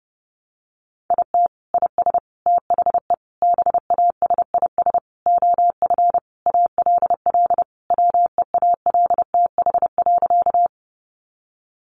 Morse code, heard as a keyed tone, about 30 wpm, 700 Hz. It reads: IT IS THE BASIS OF ALL WEALTH.
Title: Sentences from Top 1200 Words 30wpm